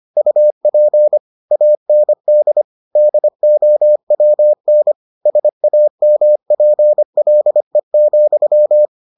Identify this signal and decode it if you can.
Morse code, heard as a keyed tone, about 25 wpm, 600 Hz. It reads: UP AND DOWN SAMPLE,